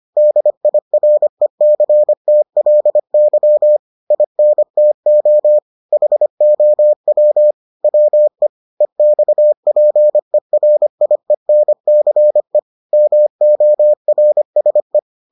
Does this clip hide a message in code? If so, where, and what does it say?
Yes: Morse code from 0.2 to 15.0 s, heard as a keyed tone, about 25 wpm, 600 Hz. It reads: DIRECTLY INTO HOW WE EXPERIENCE MORSE